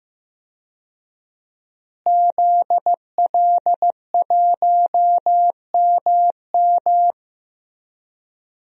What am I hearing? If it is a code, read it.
Morse code, heard as a keyed tone, about 15 wpm, 700 Hz. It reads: ZL1MM
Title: DXCC-100 Call Signs - 5 Characters - Repeated 3x - 15wpm